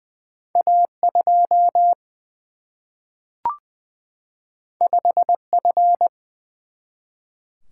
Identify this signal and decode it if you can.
Morse code, heard as a keyed tone, about 20 wpm, 700 Hz. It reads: A2 5F
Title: Single-Letter Single-Number 20wpm